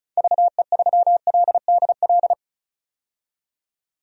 Morse code, heard as a keyed tone, about 34 wpm, 700 Hz. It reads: VE3LDL
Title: DXCC-100 Call Signs - 6 Characters - 35wpm